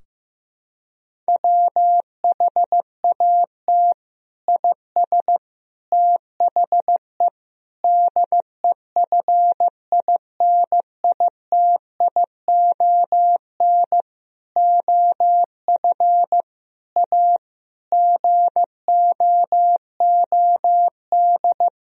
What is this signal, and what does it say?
Morse code, heard as a keyed tone, about 16 wpm, 700 Hz. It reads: WHAT IS THE DEFINITION OF A GOOD
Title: Sentences from Top 1000 Words - Review Part 4 - 15wpm